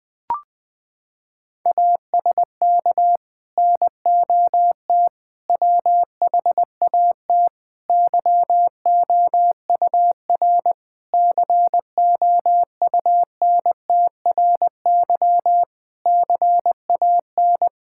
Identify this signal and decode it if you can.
Morse code, heard as a keyed tone, about 20 wpm, 700 Hz. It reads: ASK NOT WHAT YOUR COUNTRY CAN
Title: Sets of 8 Words - 20wpm